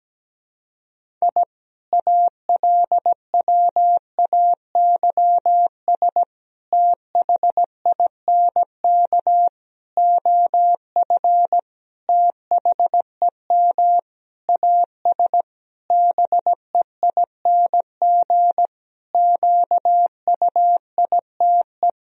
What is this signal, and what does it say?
Morse code, heard as a keyed tone, about 17 wpm, 700 Hz. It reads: I ALWAYS THINK OF THEM AS BEING QUITE
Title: Sentences from Top 1000 Words - Review Part 3 - 17wpm